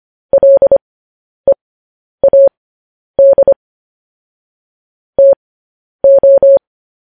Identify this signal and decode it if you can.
Morse code, heard as a keyed tone, about 25 wpm, spaced as for 10 wpm, 550 Hz. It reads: LEAD TO
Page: news-headlines-in-morse-code-at-10-wpm